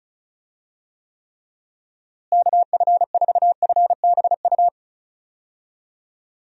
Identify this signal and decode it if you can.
Morse code, heard as a keyed tone, about 36 wpm, 700 Hz. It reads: KF4FBU
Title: USA Call Signs - 6 Characters - Repeated 2x - 35wpm